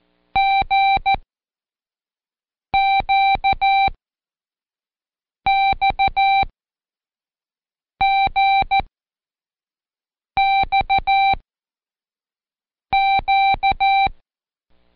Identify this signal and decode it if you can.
Morse code, heard as a keyed tone, about 14 wpm, spaced as for 5 wpm, 780 Hz. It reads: GQXGXQ
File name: Code Quick / AudioDownld1 / Seg54-Day6-Pract GQX